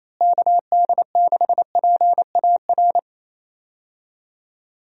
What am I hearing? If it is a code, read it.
Morse code, heard as a keyed tone, about 28 wpm, 700 Hz. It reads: KD6PAR
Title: USA Call Signs - 6 Characters - Repeated 3x - 28wpm